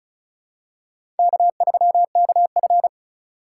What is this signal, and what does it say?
Morse code, heard as a keyed tone, about 36 wpm, 700 Hz. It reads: K3KF
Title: USA Call Signs - 2x Character Spacing - 35wpm